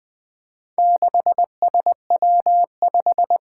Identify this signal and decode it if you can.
Morse code, heard as a keyed tone, about 20 wpm, 700 Hz. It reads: 6SW5
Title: Four Challenging Letter-Number Combinations 20wpm